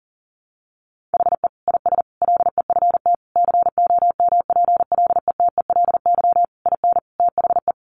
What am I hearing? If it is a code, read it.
Morse code, heard as a keyed tone, about 37 wpm, 700 Hz. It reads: HE IS LEFT COMPLETELY IN THE
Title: Sentences from Top 1100 Words 40wpm